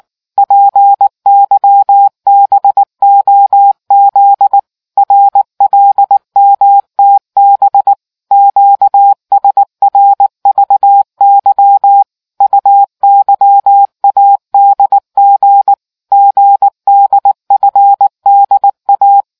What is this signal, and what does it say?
Morse code, heard as a keyed tone, about 19 wpm, 800 Hz. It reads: PYBOZ RLMTB QSRVY UYADG GDFDA